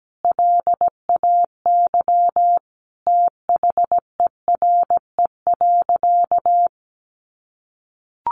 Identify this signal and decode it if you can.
Morse code, heard as a keyed tone, about 17 wpm, 700 Hz. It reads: LAY THERE.
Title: Sentences from Top 1500 Words 17wpm